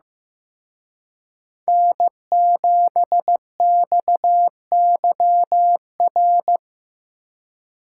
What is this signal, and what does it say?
Morse code, heard as a keyed tone, about 18 wpm, 700 Hz. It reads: N7XYR
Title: Speed Racing - USA Call Signs 15wpm